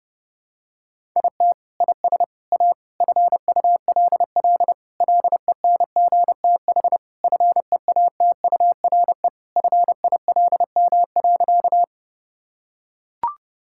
Morse code, heard as a keyed tone, about 28 wpm, 700 Hz. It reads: IT IS A FULL LENGTH FEATURE FILM.
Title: Sentences from Top 1000 Words - Review Part 5 - 30wpm